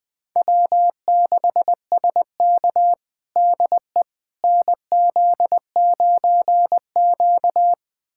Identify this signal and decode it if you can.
Morse code, heard as a keyed tone, about 20 wpm, 700 Hz. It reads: W6SK DE NZ9Q
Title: USA Callsign DE Callsign - 4 Characters - 20wpm